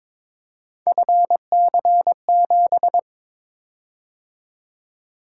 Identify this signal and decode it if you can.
Morse code, heard as a keyed tone, about 23 wpm, 700 Hz. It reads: FC7
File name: practice / 22wpm / Three Letter-Number Combinations - 22wpm